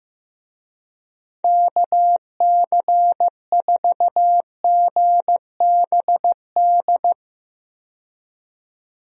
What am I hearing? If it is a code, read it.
Morse code, heard as a keyed tone, about 15 wpm, 700 Hz. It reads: KC4GBD
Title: USA Call Signs - 6 Characters - 15wpm